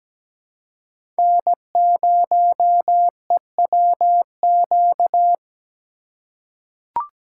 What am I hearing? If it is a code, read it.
Morse code, heard as a keyed tone, about 17 wpm, 700 Hz. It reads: N0EWQ
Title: USA Call Signs - 2x Character Spacing - 17wpm